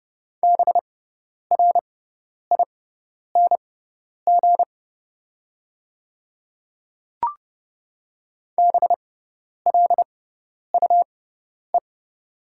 Morse code, heard as a keyed tone, about 30 wpm, spaced as for 10 wpm, 690 Hz. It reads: BRING BLUE
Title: 6x Character Spacing - Top 500 Words - 30wpm